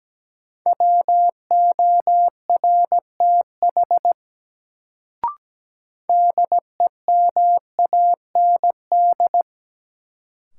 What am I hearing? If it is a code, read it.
Morse code, heard as a keyed tone, about 17 wpm, 700 Hz. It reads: WORTH DEMAND
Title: Sentences from Top 800 Words - New Word Drill - 17wpm